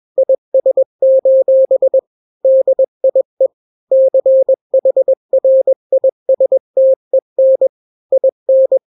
Morse code, heard as a keyed tone, about 21 wpm, 530 Hz. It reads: IS: DIE CHRISTEN IN
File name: daily / 20wpm